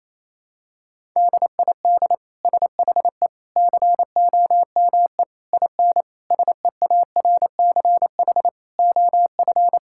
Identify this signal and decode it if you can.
Morse code, heard as a keyed tone, about 28 wpm, 700 Hz. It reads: DID SHE COME IN SEARCH OF
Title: Sentences from Top 1000 Words - Review Part 5 - 28wpm